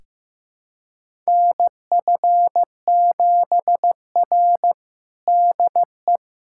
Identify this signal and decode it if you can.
Morse code, heard as a keyed tone, about 15 wpm, 700 Hz. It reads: NF7R DE
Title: USA Callsign DE Callsign - 4 Characters - 15wpm